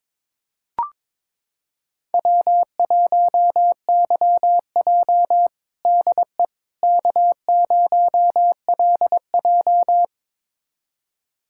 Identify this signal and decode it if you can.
Morse code, heard as a keyed tone, about 22 wpm, 700 Hz. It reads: W1YJ DE K0LJ
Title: USA Callsign DE Callsign - 4 Characters - 22wpm